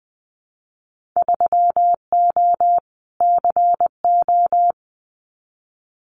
Morse code, heard as a keyed tone, about 20 wpm, 700 Hz. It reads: 3O CO
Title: Winter Field Day Exchanges Encore 20wpm